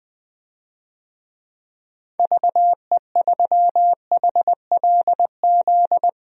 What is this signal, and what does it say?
Morse code, heard as a keyed tone, about 20 wpm, 700 Hz. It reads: VE3HLZ
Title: DXCC-100 Call Signs - 6 Characters - 20wpm